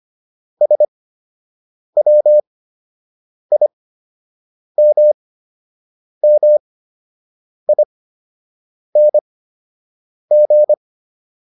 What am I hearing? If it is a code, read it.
Morse code, heard as a keyed tone, about 25 wpm, spaced as for 7 wpm, 600 Hz. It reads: SWIMMING